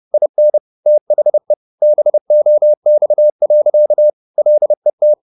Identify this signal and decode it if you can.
Morse code, heard as a keyed tone, about 30 wpm, 600 Hz. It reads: IN THE BOX. LET
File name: the old man and the sea / the old man and the sea 1 30wpm